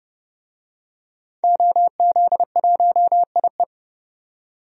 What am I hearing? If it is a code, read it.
Morse code, heard as a keyed tone, about 29 wpm, 700 Hz. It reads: OZ1IE
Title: DXCC-100 Call Signs - 5 Characters - 30wpm